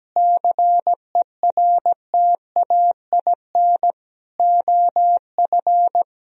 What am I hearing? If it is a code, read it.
Morse code, heard as a keyed tone, about 17 wpm, 700 Hz. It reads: CERTAIN OF
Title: Sentences from Top 1000 Words - Review Part 3 - 17wpm